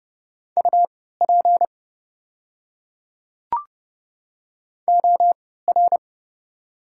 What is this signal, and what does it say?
Morse code, heard as a keyed tone, about 30 wpm, spaced as for 17 wpm, 700 Hz. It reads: UP OR
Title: 3x Character Spacing - Top 300 Words - 30wpm